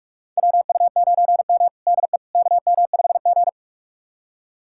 Morse code, heard as a keyed tone, about 46 wpm, 700 Hz. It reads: WU9M DE KM5G
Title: USA Callsign DE Callsign - 4 Characters - 45wpm